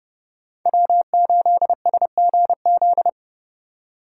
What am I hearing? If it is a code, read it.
Morse code, heard as a keyed tone, about 29 wpm, 700 Hz. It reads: W8SGZ
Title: USA Call Signs - 5 Characters - 3x Character Spacing - 30wpm